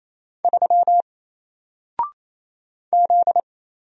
Morse code, heard as a keyed tone, about 28 wpm, 700 Hz. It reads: 3 Z
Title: Single Letter-Number Alt 28wpm